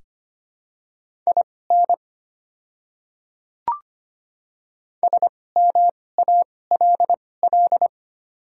Morse code, heard as a keyed tone, about 25 wpm, spaced as for 18 wpm, 700 Hz. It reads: IN SMALL
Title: 2x Character Spacing - Top 200 Words - 25wpm